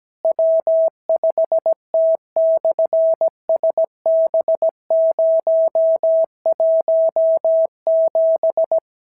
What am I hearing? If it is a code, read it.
Morse code, heard as a keyed tone, about 17 wpm, 650 Hz. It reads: W5T/SB017
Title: SOTA QSO Practice 17wpm